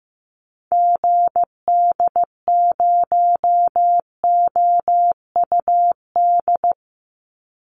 Morse code, heard as a keyed tone, about 15 wpm, 700 Hz. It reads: GD0OUD
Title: Call Signs Repeated 2x - 15wpm